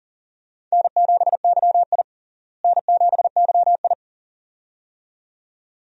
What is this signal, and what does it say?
Morse code, heard as a keyed tone, about 40 wpm, 700 Hz. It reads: N7YI N7YI
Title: USA Call Signs - 4 Characters - Repeated 2x - 40wpm